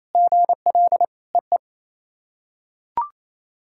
Morse code, heard as a keyed tone, about 27 wpm, 690 Hz. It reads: GL EE
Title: SOTA QSO Practice 28wpm